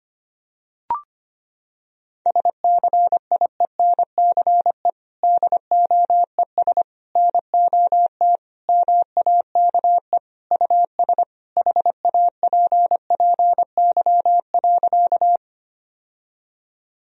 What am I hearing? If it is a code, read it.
Morse code, heard as a keyed tone, about 25 wpm, 700 Hz. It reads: SCIENCE DOES NOT MAKE US HAPPY.
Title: Sentences from Top 1000 Words - Review Part 2 - 25wpm